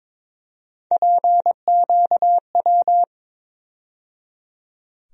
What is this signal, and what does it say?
Morse code, heard as a keyed tone, about 22 wpm, 700 Hz. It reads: PQW
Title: USA Call Sign Postfixes - 3 Letters 22wpm